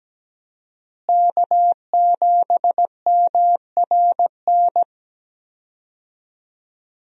Morse code, heard as a keyed tone, about 17 wpm, 700 Hz. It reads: K7MRN
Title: USA Call Signs - 5 Characters - Repeated 3x - 17wpm